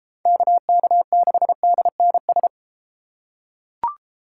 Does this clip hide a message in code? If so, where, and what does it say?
Yes: Morse code from 0.3 to 2.5 s, heard as a keyed tone, about 33 wpm, 700 Hz. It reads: KK6DNS